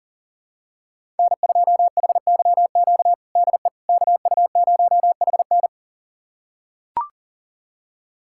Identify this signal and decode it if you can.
Morse code, heard as a keyed tone, about 37 wpm, 700 Hz. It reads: N2HYQ DE KU0HN
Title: USA Callsign DE Callsign - 5 Characters - 40wpm